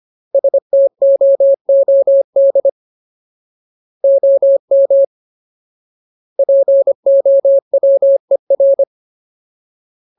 Morse code, heard as a keyed tone, about 25 wpm, 550 Hz. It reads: STOOD OM POWER
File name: Ultimate Word List / 25WPM Ultimate